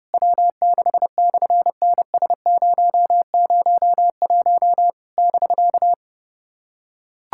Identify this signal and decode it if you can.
Morse code, heard as a keyed tone, about 28 wpm, 700 Hz. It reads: W6/NS001 <BK>
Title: SOTA QSO Practice 30wpm